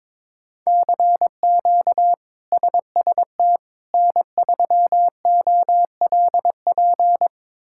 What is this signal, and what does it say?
Morse code, heard as a keyed tone, about 22 wpm, 700 Hz. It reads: CQ SST N3OLP
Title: K1USN SST QSO Practice - 22wpm